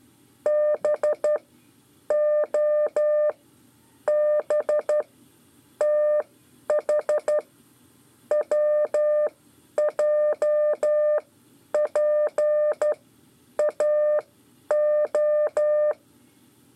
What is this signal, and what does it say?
Morse code, heard as a keyed tone, about 12 wpm, spaced as for 8 wpm, 600 Hz. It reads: BOBTHWJPAO
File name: 5 LEZIONE RIPASSO 1